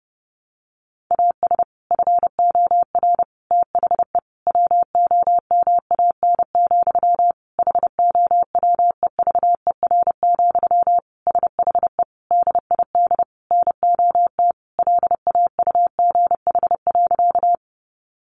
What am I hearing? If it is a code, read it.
Morse code, heard as a keyed tone, about 30 wpm, 700 Hz. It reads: AS FOR THE WOMAN, HOWEVER, SHE DID NOT LAUGH.
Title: Sentences from Top 1100 Words 30wpm